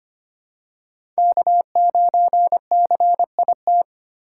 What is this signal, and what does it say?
Morse code, heard as a keyed tone, about 25 wpm, 700 Hz. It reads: K9CIT
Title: USA Call Signs - 5 Characters - 4x Character Spacing - 25wpm